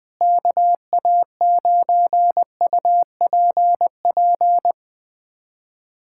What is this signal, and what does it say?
Morse code, heard as a keyed tone, about 20 wpm, 700 Hz. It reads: KA9UPP
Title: USA Callsign DE Callsign 20wpm